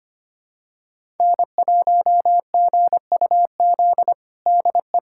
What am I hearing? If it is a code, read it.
Morse code, heard as a keyed tone, about 25 wpm, 700 Hz. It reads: N1GUZ DE
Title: USA Callsign DE Callsign 25wpm